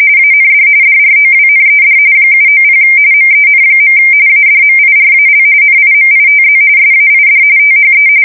RTTY, about 45.45 baud. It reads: AND DROWNED 790 MEN. ADMIRAL NIMITZ SAID THE
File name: fsk170 / battleshipnj